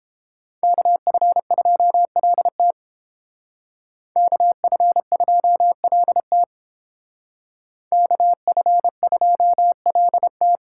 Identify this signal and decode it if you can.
Morse code, heard as a keyed tone, about 28 wpm, 700 Hz. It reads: KF2LT KF2LT KF2LT